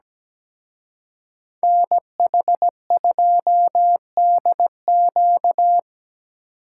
Morse code, heard as a keyed tone, about 20 wpm, 700 Hz. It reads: NH2DQ
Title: Speed Racing - USA Call Signs 17wpm